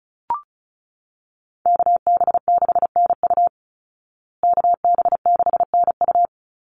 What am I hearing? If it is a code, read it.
Morse code, heard as a keyed tone, about 34 wpm, 700 Hz. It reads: KB6NU KB6NU
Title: Call Signs Repeated 2x - 35wpm